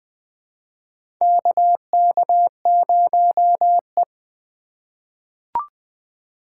Morse code, heard as a keyed tone, about 20 wpm, 700 Hz. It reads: KK0E
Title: USA Call Signs - 4 Characters - 3x Character Spacing - 20wpm